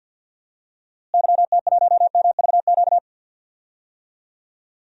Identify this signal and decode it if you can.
Morse code, heard as a keyed tone, about 48 wpm, 700 Hz. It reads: YT1MVQ
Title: DXCC-100 Call Signs - 6 Characters - Repeated 3x - 50wpm